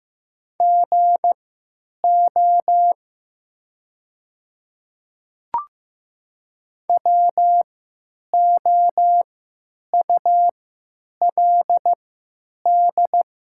Morse code, heard as a keyed tone, about 16 wpm, spaced as for 9 wpm, 700 Hz. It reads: GO WOULD
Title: 3x Character Spacing - Top 100 Words - 15wpm